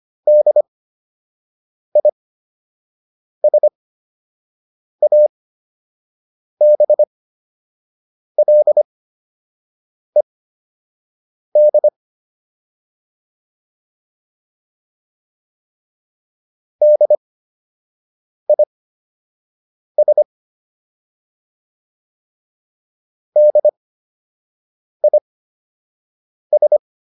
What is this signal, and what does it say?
Morse code, heard as a keyed tone, about 25 wpm, spaced as for 6 wpm, 600 Hz. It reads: DISABLED DIS DIS
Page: fundamental-practice-files